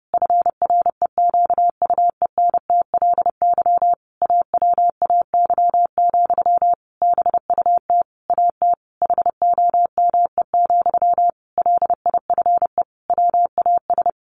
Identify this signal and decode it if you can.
Morse code, heard as a keyed tone, about 30 wpm, 700 Hz. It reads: FREQUENTLY AWAY, BUT AT HOME, LIFE WAS
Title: Sentences from Top 1400 Words 30wpm